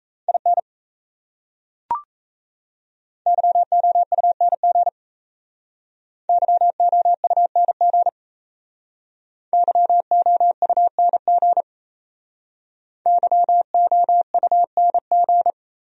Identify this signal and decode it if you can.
Morse code, heard as a keyed tone, about 38 wpm, 700 Hz. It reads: IN YOUNG YOUNG YOUNG YOUNG